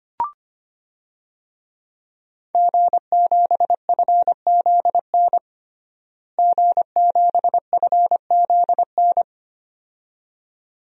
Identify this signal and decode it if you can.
Morse code, heard as a keyed tone, about 25 wpm, 700 Hz. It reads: G7FZN G7FZN
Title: DXCC-100 Call Signs - 5 Characters - Repeated 2x - 25wpm